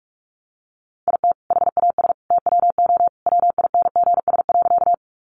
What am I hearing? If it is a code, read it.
Morse code, heard as a keyed tone, about 44 wpm, 700 Hz. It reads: IT HAS TWO WINGS.
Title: Additional Easy Phrases 45wpm